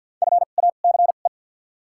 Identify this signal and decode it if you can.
Morse code, heard as a keyed tone, about 48 wpm, spaced as for 35 wpm, 700 Hz. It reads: FACE
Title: 2x Character Spacing - Top 300 Words - 50wpm